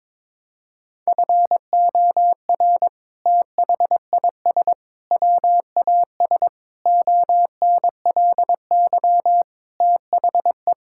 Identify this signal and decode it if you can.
Morse code, heard as a keyed tone, about 22 wpm, 700 Hz. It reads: FOR THIS WAS ONLY THE
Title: Sentences from Top 1000 Words - Review Part 5 - 22wpm